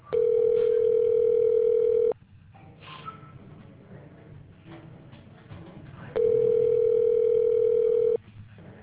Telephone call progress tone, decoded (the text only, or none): ringback tone